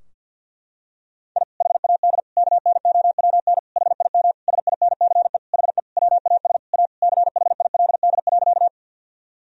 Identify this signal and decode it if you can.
Morse code, heard as a keyed tone, about 50 wpm, 700 Hz. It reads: I HAD KNOWN HIM SINCE HE WAS A CHILD.
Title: Sentences from Top 1000 Words - Review Part 3 - 50wpm